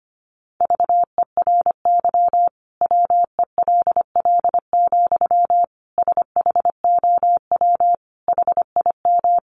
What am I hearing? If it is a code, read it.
Morse code, heard as a keyed tone, about 25 wpm, 700 Hz. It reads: VERY WELL, SHOW HIM
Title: Sentences from Top 200 Words 25wpm